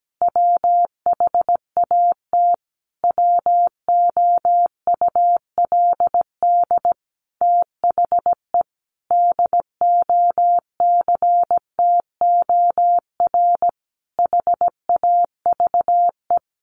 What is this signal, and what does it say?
Morse code, heard as a keyed tone, about 17 wpm, 700 Hz. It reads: WHAT WOULD THE DOCTOR HAVE
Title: Sentences from Top 1500 Words 17wpm